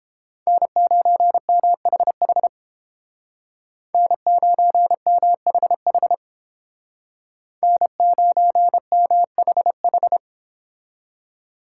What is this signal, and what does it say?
Morse code, heard as a keyed tone, about 29 wpm, 700 Hz. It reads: N9MHH N9MHH N9MHH